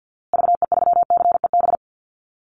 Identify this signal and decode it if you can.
Morse code, heard as a keyed tone, about 48 wpm, 700 Hz. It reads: VE3CEB
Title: Call Signs Repeated 2x - 50wpm